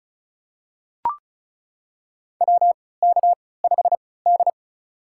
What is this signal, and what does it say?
Morse code, heard as a keyed tone, about 36 wpm, spaced as for 20 wpm, 700 Hz. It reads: WK5D
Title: USA Call Signs - 3x Character Spacing - 35wpm